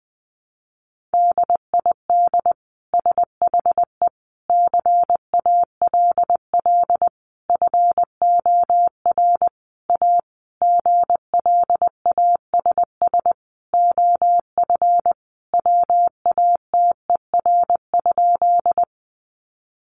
Morse code, heard as a keyed tone, about 20 wpm, 700 Hz. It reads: DID SHE CALL FOR A GLASS OF WATER?
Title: Sentences from Top 1300 Words 20wpm